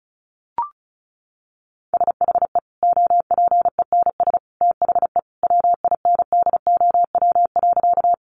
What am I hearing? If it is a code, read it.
Morse code, heard as a keyed tone, about 34 wpm, 700 Hz. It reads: SHE OPENS THE WINDOW.
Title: Additional Easy Phrases 35wpm